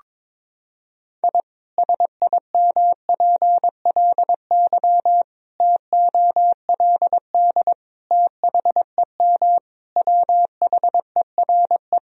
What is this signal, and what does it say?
Morse code, heard as a keyed tone, about 21 wpm, 700 Hz. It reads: I SIMPLY TOLD THEM WHERE
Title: Sentences from Top 1000 Words - Review Part 4 - 22wpm